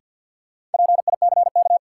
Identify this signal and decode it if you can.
Morse code, heard as a keyed tone, about 53 wpm, 700 Hz. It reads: PICK
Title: Sentences from Top 1000 Words - New Word Drill - 50wpm